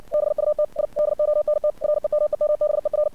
Morse code, heard as a keyed tone, about 60 wpm, 600 Hz. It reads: BUT I DONT REMEMBER
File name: archived / CRSnet